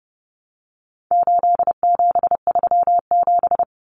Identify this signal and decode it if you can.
Morse code, heard as a keyed tone, about 30 wpm, 700 Hz. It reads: 8737
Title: Common Cut Numbers - 4-Digit 30wpm